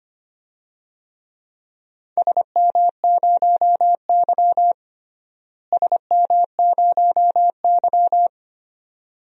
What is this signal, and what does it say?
Morse code, heard as a keyed tone, about 25 wpm, 700 Hz. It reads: SM0Y SM0Y
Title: DXCC-100 Call Signs - 4 Characters - Repeated 2x - 25wpm